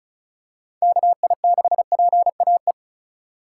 Morse code, heard as a keyed tone, about 34 wpm, 700 Hz. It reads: KI6PAE
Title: USA Call Signs - 6 Characters - 3x Character Spacing - 35wpm